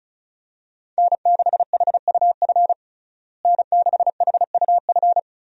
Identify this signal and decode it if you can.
Morse code, heard as a keyed tone, about 34 wpm, 700 Hz. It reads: N6HUF N6HUF
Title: USA Call Signs - 5 Characters - Repeated 2x - 35wpm